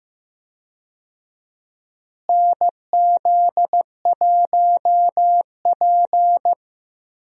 Morse code, heard as a keyed tone, about 15 wpm, 700 Hz. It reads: NZ1P
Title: USA Call Signs - 4 Characters - Repeated 2x - 15wpm